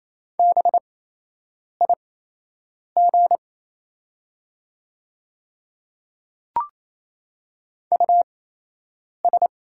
Morse code, heard as a keyed tone, about 28 wpm, spaced as for 8 wpm, 700 Hz. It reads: BIG US
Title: 8x Character Spacing - Top 200 Words - 28wpm